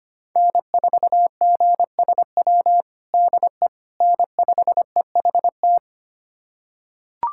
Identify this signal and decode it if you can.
Morse code, heard as a keyed tone, about 25 wpm, 700 Hz. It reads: N4GSW DE N5EHT
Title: USA Callsign DE Callsign - 5 Characters - 25wpm